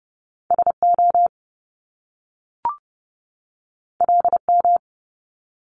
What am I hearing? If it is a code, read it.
Morse code, heard as a keyed tone, about 30 wpm, 700 Hz. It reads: SO LM
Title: Top Two Letter Combinations in Words 30wpm